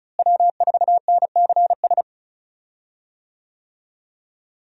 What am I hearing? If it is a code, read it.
Morse code, heard as a keyed tone, about 36 wpm, 700 Hz. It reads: W4NCS
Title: USA Call Signs - 5 Characters - Repeated 3x - 35wpm